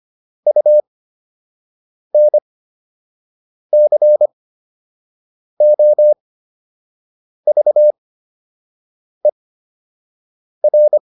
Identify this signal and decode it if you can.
Morse code, heard as a keyed tone, about 25 wpm, spaced as for 6 wpm, 600 Hz. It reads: UNCOVER
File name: UN7 06